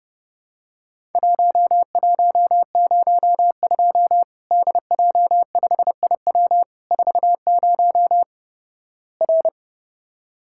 Morse code, heard as a keyed tone, about 30 wpm, 700 Hz. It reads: 1102 DJ5IW 40 R
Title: Worked All Europe DX Contest - QTC Exchange - 30wpm